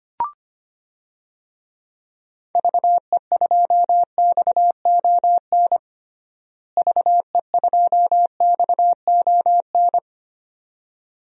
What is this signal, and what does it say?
Morse code, heard as a keyed tone, about 25 wpm, 700 Hz. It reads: VE2XON VE2XON
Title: DXCC-100 Call Signs - 6 Characters - Repeated 2x - 25wpm